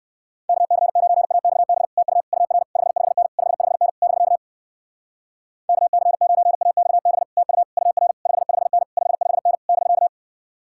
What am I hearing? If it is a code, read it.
Morse code, heard as a keyed tone, about 61 wpm, 700 Hz. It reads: KC9AXB TU UR 55N 55N <BK> KC9AXB TU UR 55N 55N <BK>